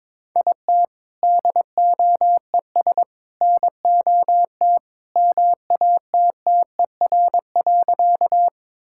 Morse code, heard as a keyed tone, about 21 wpm, 700 Hz. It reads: IT DOES NOT MATTER.
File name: practice / 22wpm / Sentences with 3-Word Phrases - 22wpm